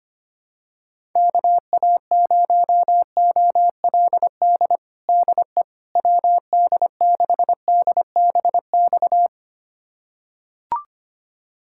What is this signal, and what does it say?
Morse code, heard as a keyed tone, about 25 wpm, 700 Hz. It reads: KA0OLD DE WD6DBX
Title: USA Callsign DE Callsign - 6 Characters - 25wpm